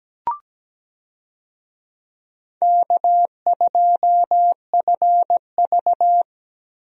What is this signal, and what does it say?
Morse code, heard as a keyed tone, about 17 wpm, 700 Hz. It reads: K2FV
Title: USA Call Signs - 4 Characters - Repeated 3x - 17wpm